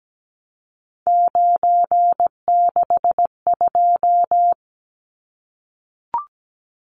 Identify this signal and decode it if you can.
Morse code, heard as a keyed tone, about 17 wpm, 700 Hz. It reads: 962
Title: Numbers - 3-Digit 17wpm